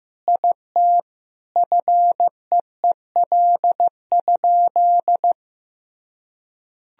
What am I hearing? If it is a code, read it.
Morse code, heard as a keyed tone, about 15 wpm, 700 Hz. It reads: IT FEEL?
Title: Sentences from Top 400 Words and Abbreviations 15wpm